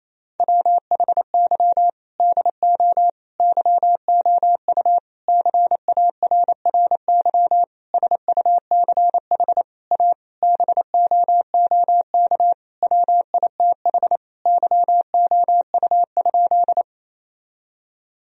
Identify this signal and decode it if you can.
Morse code, heard as a keyed tone, about 28 wpm, 700 Hz. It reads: WHY DO YOU CARRY SUCH A BOOK WITH YOU?
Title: Sentences from Top 1000 Words - Review Part 1 - 28wpm